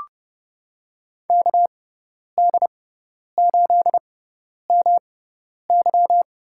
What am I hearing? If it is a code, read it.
Morse code, heard as a keyed tone, about 30 wpm, spaced as for 10 wpm, 700 Hz. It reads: KD8MY
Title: USA Call Signs - 6x Character Spacing - 30wpm